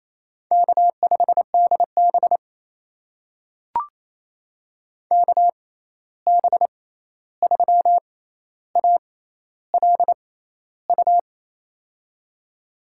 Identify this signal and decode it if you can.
Morse code, heard as a keyed tone, about 28 wpm, spaced as for 13 wpm, 700 Hz. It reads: K5DB KB3ALU
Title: USA Call Signs - 6x Character Spacing - 28wpm